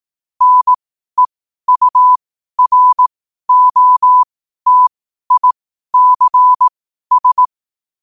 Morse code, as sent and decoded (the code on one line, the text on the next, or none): -. ..- .-. --- - .. -.-. ...
NEUROTICS